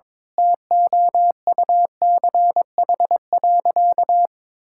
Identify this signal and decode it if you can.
Morse code, heard as a keyed tone, about 21 wpm, 700 Hz. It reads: TOUCH.
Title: Sentences from Top 1600 Words 22wpm